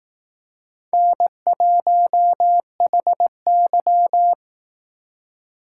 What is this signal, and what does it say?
Morse code, heard as a keyed tone, about 18 wpm, 700 Hz. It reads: N1HY
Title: Speed Racing - USA 1x2 Call Signs 15wpm